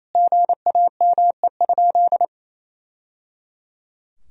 Morse code, heard as a keyed tone, about 28 wpm, 700 Hz. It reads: GAME?
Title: Sentences from Top 1000 Words - Review Part 4 - 28wpm